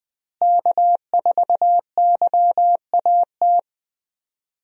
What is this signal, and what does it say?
Morse code, heard as a keyed tone, about 20 wpm, 700 Hz. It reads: K4YAT
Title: USA Call Signs - 5 Characters - Repeated 2x - 20wpm